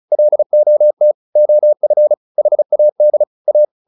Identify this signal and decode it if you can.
Morse code, heard as a keyed tone, about 34 wpm, 600 Hz. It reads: LOT OF HAD A